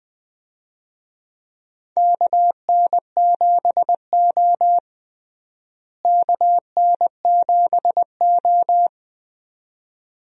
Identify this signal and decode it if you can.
Morse code, heard as a keyed tone, about 20 wpm, 700 Hz. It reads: KN7O KN7O
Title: USA Call Signs - 4 Characters - Repeated 2x - 20wpm